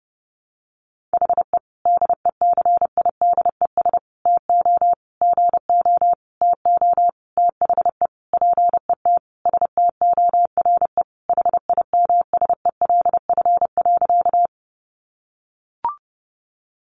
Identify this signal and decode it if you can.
Morse code, heard as a keyed tone, about 28 wpm, 700 Hz. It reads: HE DECIDES TO GO TO THE PET STORE HIMSELF.
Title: Easy Phrases 30wpm